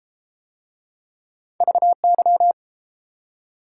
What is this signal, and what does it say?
Morse code, heard as a keyed tone, about 32 wpm, 700 Hz. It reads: VY